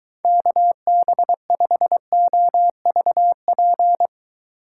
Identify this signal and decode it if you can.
Morse code, heard as a keyed tone, about 23 wpm, 700 Hz. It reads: KB5OVP